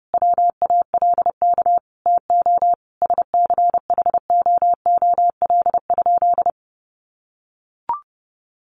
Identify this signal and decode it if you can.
Morse code, heard as a keyed tone, about 30 wpm, 700 Hz. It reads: WALK TO SCHOOL?
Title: Sentences from Top 1400 Words 30wpm